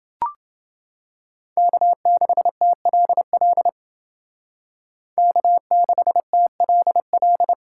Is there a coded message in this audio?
Yes: Morse code from 1.6 to 7.5 s, heard as a keyed tone, about 29 wpm, 700 Hz. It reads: K6TLL K6TLL